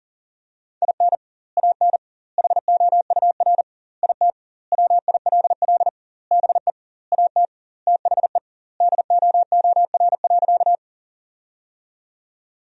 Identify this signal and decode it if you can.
Morse code, heard as a keyed tone, about 37 wpm, 700 Hz. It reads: IN AN HOUR IT WILL BE AT THE DOOR.
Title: 2x Word-Spacing - Sentences from Top 400 Words - 40wpm